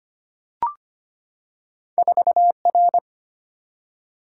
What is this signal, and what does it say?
Morse code, heard as a keyed tone, about 25 wpm, 700 Hz. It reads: 4R
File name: Two Letter-Number Combinations - 25wpm